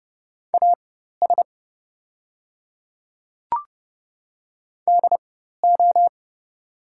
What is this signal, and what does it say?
Morse code, heard as a keyed tone, about 30 wpm, spaced as for 14 wpm, 700 Hz. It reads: AS DO
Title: 4x Character Spacing - Top 100 Words - 30wpm